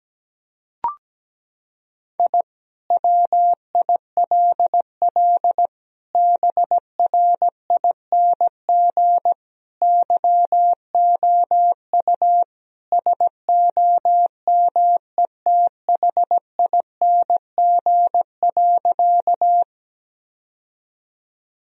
Morse code, heard as a keyed tone, about 17 wpm, 700 Hz. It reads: I WILL BRING YOU SOMETHING.
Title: Sentences from Top 1000 Words - Review Part 3 - 17wpm